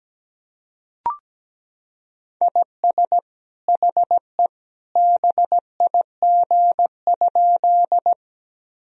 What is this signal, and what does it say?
Morse code, heard as a keyed tone, about 17 wpm, 700 Hz. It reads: IS HE BIG?
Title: Sentences from Top 1000 Words - Review Part 1 - 17wpm